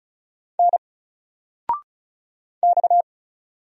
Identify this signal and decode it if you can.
Morse code, heard as a keyed tone, about 36 wpm, 700 Hz. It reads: N X
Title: Single Letter-Number 35wpm